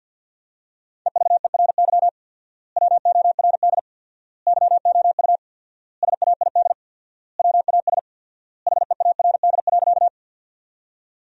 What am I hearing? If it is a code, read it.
Morse code, heard as a keyed tone, about 50 wpm, 700 Hz. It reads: EVERY WORD YOU SAID WAS HEARD.
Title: 4x Word-Spacing - Sentences from Top 400 Words - 50wpm